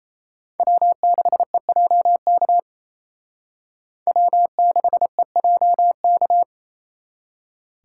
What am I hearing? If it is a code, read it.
Morse code, heard as a keyed tone, about 35 wpm, 700 Hz. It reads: W6EJK W6EJK